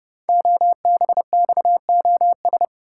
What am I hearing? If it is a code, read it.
Morse code, heard as a keyed tone, about 30 wpm, 700 Hz. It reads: OBXOS
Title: Code Groups - 5 Letters - 30wpm